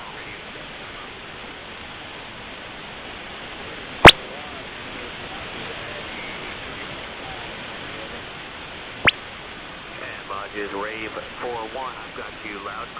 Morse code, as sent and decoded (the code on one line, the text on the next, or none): none